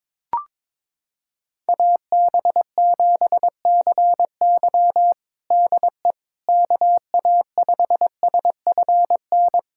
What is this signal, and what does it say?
Morse code, heard as a keyed tone, about 21 wpm, 700 Hz. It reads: AB7CY DE KA5SFN
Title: USA Callsign DE Callsign 22wpm